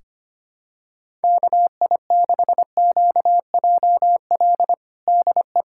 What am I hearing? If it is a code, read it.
Morse code, heard as a keyed tone, about 25 wpm, 700 Hz. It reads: KI6QJL DE
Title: USA Callsign DE Callsign - 6 Characters - 25wpm